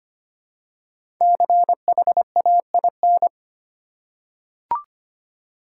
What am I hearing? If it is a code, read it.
Morse code, heard as a keyed tone, about 25 wpm, 700 Hz. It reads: CHAIN
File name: 5 Letter Words - First 3 Letters Given - 25wpm